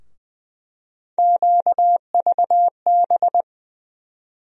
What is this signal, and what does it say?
Morse code, heard as a keyed tone, about 20 wpm, 700 Hz. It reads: QVB
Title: USA Call Sign Postfixes - 3 Letters 20wpm